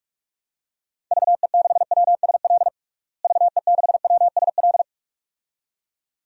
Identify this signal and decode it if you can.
Morse code, heard as a keyed tone, about 46 wpm, 700 Hz. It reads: VE6WSL VE6WSL
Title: DXCC-100 Call Signs - 6 Characters - Repeated 2x - 45wpm